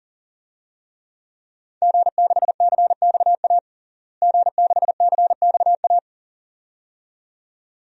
Morse code, heard as a keyed tone, about 40 wpm, 700 Hz. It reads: G6CXA G6CXA
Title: DXCC-100 Call Signs - 5 Characters - Repeated 2x - 40wpm